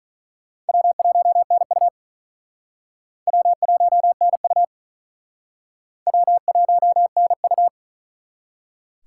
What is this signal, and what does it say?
Morse code, heard as a keyed tone, about 43 wpm, 700 Hz. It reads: W1NU W1NU W1NU
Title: Speed Racing - USA 1x2 Call Signs 35wpm